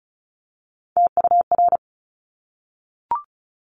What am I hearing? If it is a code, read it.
Morse code, heard as a keyed tone, about 36 wpm, 700 Hz. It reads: TUR
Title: Top Three Letter Combinations in Words 35wpm